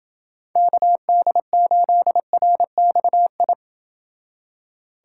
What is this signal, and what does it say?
Morse code, heard as a keyed tone, about 28 wpm, 700 Hz. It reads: KD8RXI